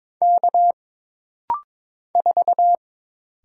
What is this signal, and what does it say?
Morse code, heard as a keyed tone, about 22 wpm, 700 Hz. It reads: K 4
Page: This rapid-fire timing is geared to push your ICR (Instant Character Recognition) skills.